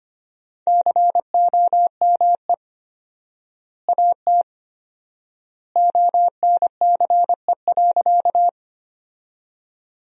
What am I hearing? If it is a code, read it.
Morse code, heard as a keyed tone, about 25 wpm, 700 Hz. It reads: COME AT ONCE.
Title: 4x Word-Spacing - Sentences from Top 300 Words - 25wpm